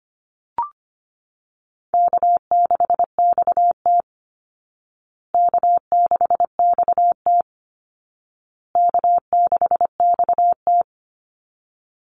Morse code, heard as a keyed tone, about 25 wpm, 700 Hz. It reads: K6XT K6XT K6XT
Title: Easy Call Signs Repeated 3x - 25wpm